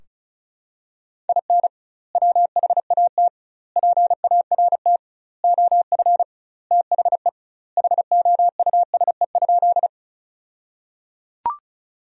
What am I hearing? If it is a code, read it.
Morse code, heard as a keyed tone, about 34 wpm, 700 Hz. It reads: IN WHAT PART OF THE HOUSE?
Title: 2x Word-Spacing - Sentences from Top 200 Words - 35wpm